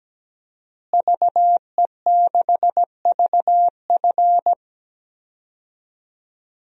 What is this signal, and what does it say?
Morse code, heard as a keyed tone, about 17 wpm, 700 Hz. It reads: VE6VF
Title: DXCC-100 Call Signs - 5 Characters - Repeated 2x - 17wpm